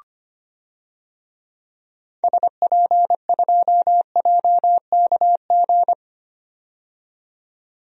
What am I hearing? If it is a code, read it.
Morse code, heard as a keyed tone, about 25 wpm, 700 Hz. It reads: SP2JKG
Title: DXCC-100 Call Signs - 6 Characters - 25wpm